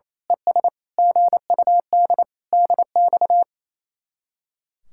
Morse code, heard as a keyed tone, about 28 wpm, 700 Hz. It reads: ES GUD DX
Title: Speed-Racing Common QSO Phrases Encore - 28wpm